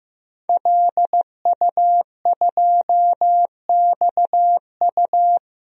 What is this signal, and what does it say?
Morse code, heard as a keyed tone, about 15 wpm, 700 Hz. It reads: LU2XU
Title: DXCC-100 Call Signs - 5 Characters - Repeated 2x - 15wpm